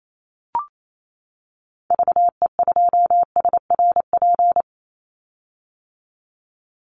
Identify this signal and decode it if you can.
Morse code, heard as a keyed tone, about 28 wpm, 700 Hz. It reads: VE2SRP
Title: Call Signs - 28wpm